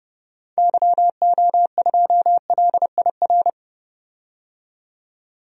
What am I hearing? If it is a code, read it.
Morse code, heard as a keyed tone, about 30 wpm, 700 Hz. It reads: YO2LIR
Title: DXCC-100 Call Signs - 6 Characters - 30wpm